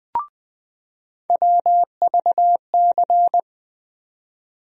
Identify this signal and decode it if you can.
Morse code, heard as a keyed tone, about 20 wpm, 700 Hz. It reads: WVC
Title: USA Call Sign Postfixes - 3 Letters 20wpm